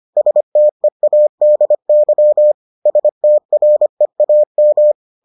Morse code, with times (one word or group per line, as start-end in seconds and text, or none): 0.2-2.5 STEADY
2.9-4.9 STREAM